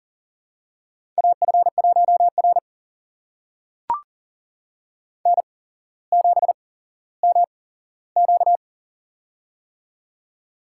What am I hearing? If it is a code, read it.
Morse code, heard as a keyed tone, about 40 wpm, spaced as for 17 wpm, 700 Hz. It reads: AF1R N7MQ
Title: USA Call Signs - 4 Characters - 8x Character Spacing - 40wpm